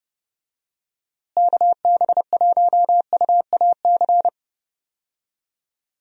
Morse code, heard as a keyed tone, about 30 wpm, 700 Hz. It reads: KB1UAC
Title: USA Call Signs - 6 Characters - Repeated 3x - 30wpm